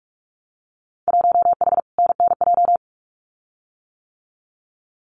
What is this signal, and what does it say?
Morse code, heard as a keyed tone, about 46 wpm, 700 Hz. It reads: 1H NNJ
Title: Winter Field Day Exchanges 45wpm